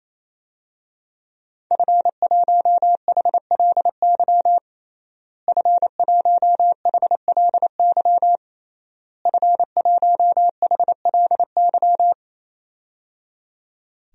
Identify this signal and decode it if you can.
Morse code, heard as a keyed tone, about 28 wpm, 700 Hz. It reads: F1HLY F1HLY F1HLY
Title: DXCC-100 Call Signs - 5 Characters - Repeated 3x - 28wpm